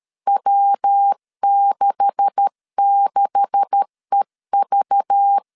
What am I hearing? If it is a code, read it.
Morse code, heard as a keyed tone, about 13 wpm, 800 Hz. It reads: W66EV
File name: LEZIONI / 9 286 12wpm 2305